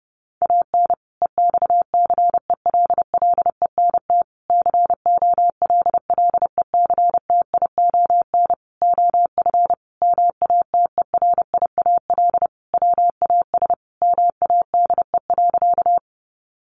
Morse code, heard as a keyed tone, about 28 wpm, 700 Hz. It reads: AN EXCELLENT COLLECTION OF MATERIAL WAS MADE.
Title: Sentences from Top 1200 Words 30wpm